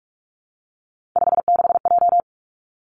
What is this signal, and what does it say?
Morse code, heard as a keyed tone, about 46 wpm, 700 Hz. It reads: 56J
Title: Three Challenging Letter-Number Combinations 45wpm